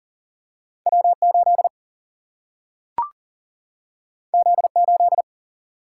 Morse code, heard as a keyed tone, about 38 wpm, 700 Hz. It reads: W8 Z8
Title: Single-Letter Single-Number 40wpm